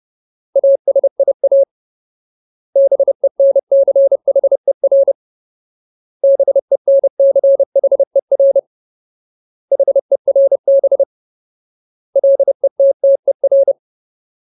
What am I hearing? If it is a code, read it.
Morse code, heard as a keyed tone, about 30 wpm, 550 Hz. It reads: ASIA BENCHER BENCHER HERB LETTER
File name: Ultimate Word List / 30WPM Ultimate